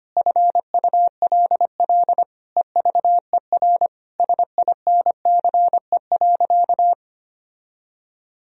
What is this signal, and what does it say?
Morse code, heard as a keyed tone, about 25 wpm, 700 Hz. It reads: FULL EVER SINCE.
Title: Sentences from Top 1000 Words - Review Part 3 - 25wpm